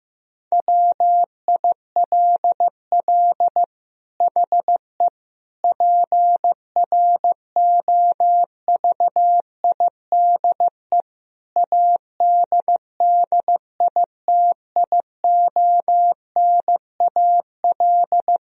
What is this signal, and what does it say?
Morse code, heard as a keyed tone, about 15 wpm, 700 Hz. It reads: WILL HE PROVIDE ADDITIONAL
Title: Sentences from Top 1000 Words - Review Part 4 - 15wpm